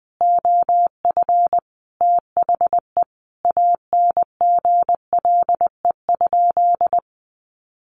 Morse code, heard as a keyed tone, about 20 wpm, 700 Hz. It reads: OF THE ANGLE?
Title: Sentences from Top 1500 Words 20wpm